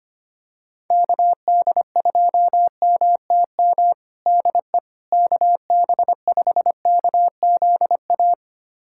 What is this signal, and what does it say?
Morse code, heard as a keyed tone, about 25 wpm, 700 Hz. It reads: KD2MTM DE KB5KZA
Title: USA Callsign DE Callsign - 6 Characters - 25wpm